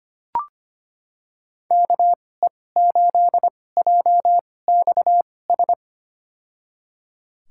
Morse code, heard as a keyed tone, about 25 wpm, spaced as for 18 wpm, 700 Hz. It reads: KE8JXS
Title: USA Call Signs - 6 Characters - 2x Character Spacing - 25wpm